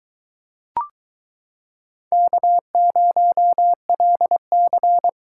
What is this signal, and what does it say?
Morse code, heard as a keyed tone, about 22 wpm, 700 Hz. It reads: K0LC